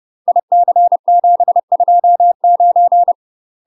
Morse code, heard as a keyed tone, about 30 wpm, 700 Hz. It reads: IC729
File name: UK MASTER QSOs TRAINING / 30WPM / UK MASTER QSOs 0001